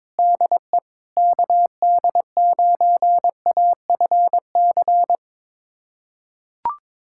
Morse code, heard as a keyed tone, about 21 wpm, 700 Hz. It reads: DE KD9AFC
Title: USA Callsign DE Callsign 22wpm